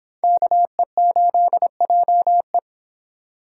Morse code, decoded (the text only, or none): KE8JE